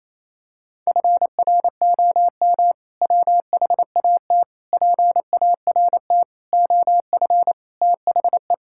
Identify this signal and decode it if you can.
Morse code, heard as a keyed tone, about 27 wpm, 700 Hz. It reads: FROM WHAT PART OF THE
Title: Sentences from Top 1000 Words - Review Part 6 - 28wpm